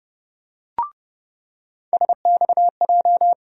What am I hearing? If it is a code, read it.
Morse code, heard as a keyed tone, about 30 wpm, 700 Hz. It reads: SXJ
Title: USA Call Sign Postfixes - 3 Letters 30wpm